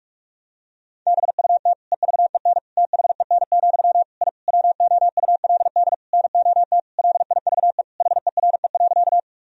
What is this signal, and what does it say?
Morse code, heard as a keyed tone, about 46 wpm, 700 Hz. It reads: BUT EVEN THEN, I WOULD NOT LIVE HERE.
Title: Sentences from Top 1000 Words - Review Part 1 - 45wpm